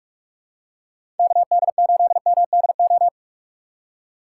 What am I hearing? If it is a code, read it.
Morse code, heard as a keyed tone, about 46 wpm, 700 Hz. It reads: KD8MDO
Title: USA Call Signs - 6 Characters - 3x Character Spacing - 45wpm